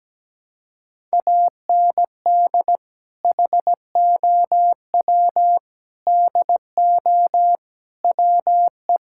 Morse code, heard as a keyed tone, about 17 wpm, 700 Hz. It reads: AND HOW DO WE
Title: Sentences from Top 1600 Words 17wpm